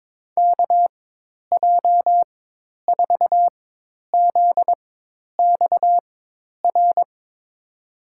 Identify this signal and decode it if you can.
Morse code, heard as a keyed tone, about 22 wpm, spaced as for 10 wpm, 700 Hz. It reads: KJ4ZXR
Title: USA Call Signs - 6 Characters - 4x Character Spacing - 22wpm